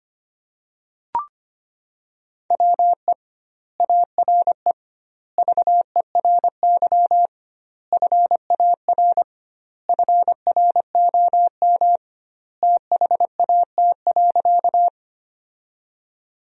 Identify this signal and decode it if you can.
Morse code, heard as a keyed tone, about 25 wpm, 700 Hz. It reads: WE ARE VERY FAR FROM THAT.
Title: 2x Word-Spacing - Sentences from Top 300 Words - 25wpm